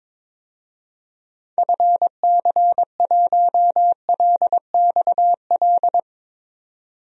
Morse code, heard as a keyed tone, about 22 wpm, 700 Hz. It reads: FC1LXL
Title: DXCC-100 Call Signs - 6 Characters - 22wpm